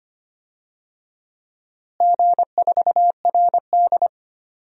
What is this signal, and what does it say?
Morse code, heard as a keyed tone, about 25 wpm, 700 Hz. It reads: G4RD
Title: DXCC-100 Call Signs - 4 Characters - Repeated 3x - 25wpm